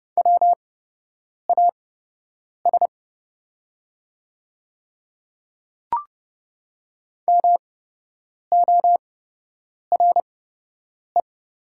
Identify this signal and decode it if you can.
Morse code, heard as a keyed tone, about 30 wpm, spaced as for 8 wpm, 700 Hz. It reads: WAS MORE
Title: 8x Character Spacing - Top 200 Words - 30wpm